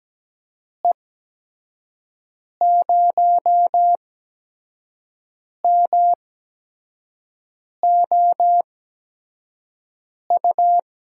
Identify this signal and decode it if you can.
Morse code, heard as a keyed tone, about 17 wpm, spaced as for 5 wpm, 700 Hz. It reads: E0MOU
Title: USA Call Signs - 6 Characters - 8x Character Spacing - 17wpm